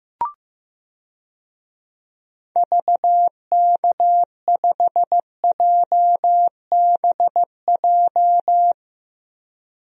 Morse code, heard as a keyed tone, about 15 wpm, 700 Hz. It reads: VK5JBJ
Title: DXCC-100 Call Signs - 6 Characters - 15wpm